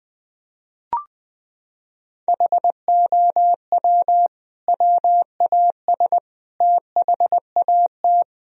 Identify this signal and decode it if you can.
Morse code, heard as a keyed tone, about 20 wpm, 700 Hz. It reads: HOW WAS THAT
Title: Sentences from Top 1000 Words - Review Part 4 - 20wpm